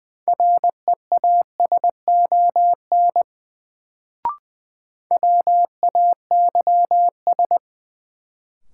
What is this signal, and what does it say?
Morse code, heard as a keyed tone, about 21 wpm, 700 Hz. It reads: REASON WAYS
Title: Sets of 5 Words Encore - New Word Drill - 20wpm